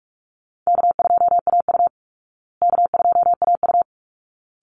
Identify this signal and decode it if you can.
Morse code, heard as a keyed tone, about 44 wpm, 700 Hz. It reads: K2AU K2AU
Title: Call Signs Repeated 3x - 45wpm